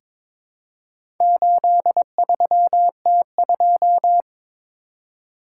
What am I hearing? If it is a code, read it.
Morse code, heard as a keyed tone, about 22 wpm, 700 Hz. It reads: 83T2
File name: practice / 22wpm / Four Letter-Number Combinations - 22wpm